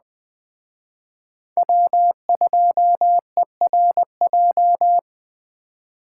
Morse code, heard as a keyed tone, about 21 wpm, 700 Hz. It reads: W2ERJ
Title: Speed Racing - USA 1x3 Call Signs 17wpm